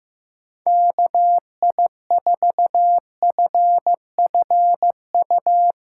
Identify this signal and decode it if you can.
Morse code, heard as a keyed tone, about 15 wpm, 700 Hz. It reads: KI4FFU
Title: USA Callsign DE Callsign 15wpm